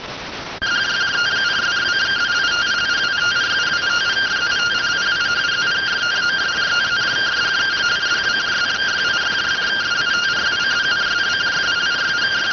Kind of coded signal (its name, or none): two-tone teleprinter signal